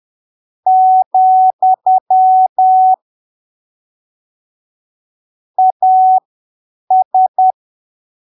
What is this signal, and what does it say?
Morse code, heard as a keyed tone, about 10 wpm, spaced as for 7 wpm, 750 Hz. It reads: , AS
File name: arquivos / texto / texto 25 10ppm